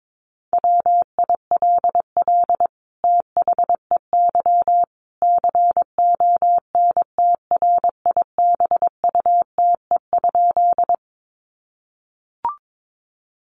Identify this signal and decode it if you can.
Morse code, heard as a keyed tone, about 22 wpm, 700 Hz. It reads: WILL THEY CONTRIBUTE?
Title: Sentences from Top 1400 Words 22wpm